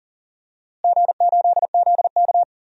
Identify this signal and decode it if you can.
Morse code, heard as a keyed tone, about 40 wpm, 700 Hz. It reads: G8ZK
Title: DXCC-100 Call Signs - 4 Characters - Repeated 2x - 40wpm